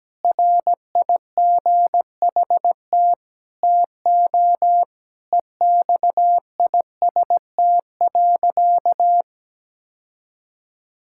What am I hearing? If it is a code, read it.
Morse code, heard as a keyed tone, about 17 wpm, 700 Hz. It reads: RIGHT TO EXIST.
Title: Sentences from Top 1000 Words - Review Part 5 - 17wpm